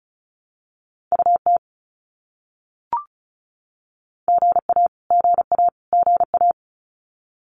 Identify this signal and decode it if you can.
Morse code, heard as a keyed tone, about 34 wpm, 700 Hz. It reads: UT GA GA GA
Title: US States Repeated 3x 35wpm